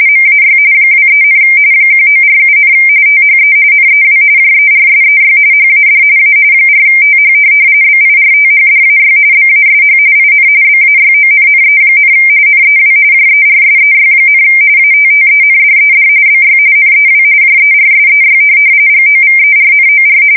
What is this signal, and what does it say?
RTTY, about 45.45 baud. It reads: MAY 23, 1947, THE NEW JERSEY HAD A 4TH BIRTHDAY BASH IN BAYONNE, NEW JERSEY, ATTENDED BY NEW JERSEY GOVERNOR
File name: fsk170 / USS-New-Jersey